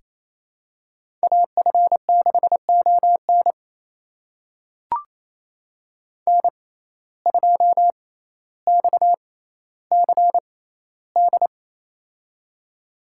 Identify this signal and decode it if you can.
Morse code, heard as a keyed tone, about 28 wpm, spaced as for 14 wpm, 700 Hz. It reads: AF6ON N2XCD
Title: USA Call Signs - 5 Characters - 6x Character Spacing - 28wpm